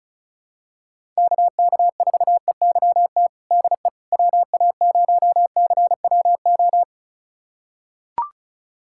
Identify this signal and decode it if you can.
Morse code, heard as a keyed tone, about 34 wpm, 700 Hz. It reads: KK4EYT DE WA0CWO
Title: USA Callsign DE Callsign - 6 Characters - 35wpm